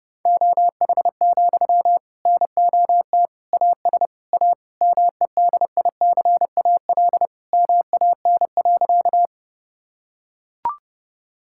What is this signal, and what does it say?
Morse code, heard as a keyed tone, about 32 wpm, 700 Hz. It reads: OH, NOT AS A MEDICAL MAN.
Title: Sentences from Top 1000 Words - Review Part 5 - 30wpm